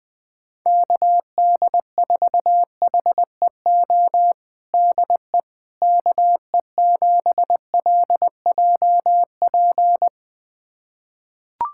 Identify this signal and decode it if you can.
Morse code, heard as a keyed tone, about 21 wpm, 700 Hz. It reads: KD4HEO DE KE7LJP
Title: USA Callsign DE Callsign 20wpm